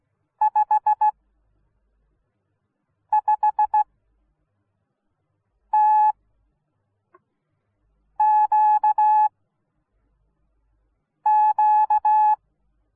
Morse code, heard as a keyed tone, about 16 wpm, spaced as for 4 wpm, 850 Hz. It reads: 55TQQ